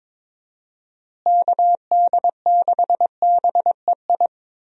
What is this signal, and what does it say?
Morse code, heard as a keyed tone, about 22 wpm, 700 Hz. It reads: KD6BEI
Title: USA Call Signs - 6 Characters - Repeated 3x - 22wpm